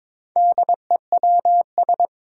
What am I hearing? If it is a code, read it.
Morse code, heard as a keyed tone, about 21 wpm, 700 Hz. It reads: DEWS
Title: Four Challenging Letter-Number Combinations 22wpm